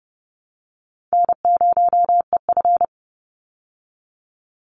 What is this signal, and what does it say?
Morse code, heard as a keyed tone, about 30 wpm, 700 Hz. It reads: N0EF
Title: Easy Call Signs Repeated 2x - 30wpm